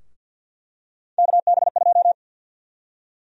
Morse code, heard as a keyed tone, about 48 wpm, 700 Hz. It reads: KB2
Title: USA Call Sign Prefixes - 50wpm